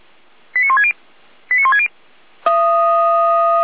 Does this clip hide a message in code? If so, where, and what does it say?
Yes: five-tone selective call (ZVEI) from 0.5 to 1.9 s, heard as quick runs of five tones, tones of about 70 ms. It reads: 89160 89160